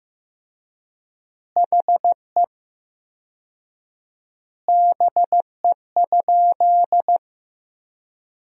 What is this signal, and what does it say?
Morse code, heard as a keyed tone, about 15 wpm, 700 Hz. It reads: HE BE?
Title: 4x Word-Spacing - Sentences from Top 200 Words - 15wpm